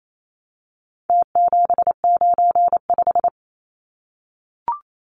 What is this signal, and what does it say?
Morse code, heard as a keyed tone, about 28 wpm, 700 Hz. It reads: T795
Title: Numbers - Contest Style - 4-Digit 28wpm